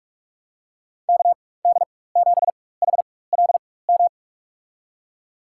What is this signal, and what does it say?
Morse code, heard as a keyed tone, about 44 wpm, spaced as for 21 wpm, 700 Hz. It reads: KD7HLM
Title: USA Call Signs - 6 Characters - 4x Character Spacing - 45wpm